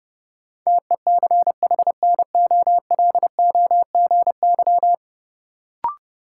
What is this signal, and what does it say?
Morse code, heard as a keyed tone, about 30 wpm, 700 Hz. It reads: TECHNOLOGY
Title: Sentences from Top 1200 Words - New Word Drill Alt - 30wpm